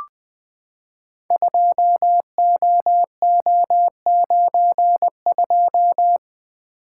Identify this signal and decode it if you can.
Morse code, heard as a keyed tone, about 20 wpm, 700 Hz. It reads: 2OO92
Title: Code Groups - 5 Letter:Numbers - 20wpm